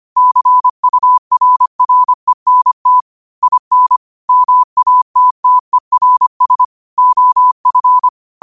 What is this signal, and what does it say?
Morse code, heard as a keyed tone, about 25 wpm, 1000 Hz. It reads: CURRENT IN MATTERS OF
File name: qotdfiles / 25wpm